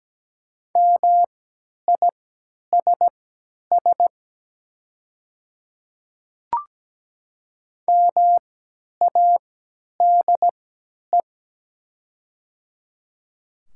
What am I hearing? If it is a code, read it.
Morse code, heard as a keyed tone, about 17 wpm, spaced as for 10 wpm, 700 Hz. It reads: MISS MADE
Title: 3x Character Spacing - Top 500 Words - 17wpm